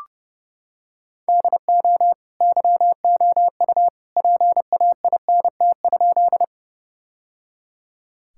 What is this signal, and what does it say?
Morse code, heard as a keyed tone, about 30 wpm, 700 Hz. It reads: DO YOU PAINT?
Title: Sentences from Top 1000 Words - Review Part 6 - 30wpm